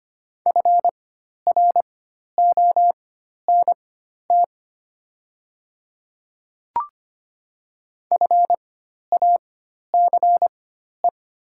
Morse code, heard as a keyed tone, about 25 wpm, spaced as for 12 wpm, 700 Hz. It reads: FRONT FACE
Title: 4x Character Spacing - Top 500 Words - 25wpm